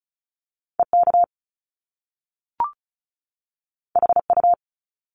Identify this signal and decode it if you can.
Morse code, heard as a keyed tone, about 36 wpm, 700 Hz. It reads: EK HU
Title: Top Two Letter Combinations in Words 35wpm